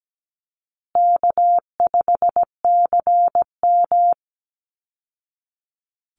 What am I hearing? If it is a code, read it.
Morse code, heard as a keyed tone, about 17 wpm, 700 Hz. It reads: K5CM
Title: Easy Call Signs Repeated 3x - 17wpm